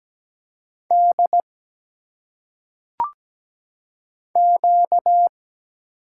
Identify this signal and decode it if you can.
Morse code, heard as a keyed tone, about 17 wpm, 700 Hz. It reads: D Q
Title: Single Letters - Phonetic Pronunciation 17wpm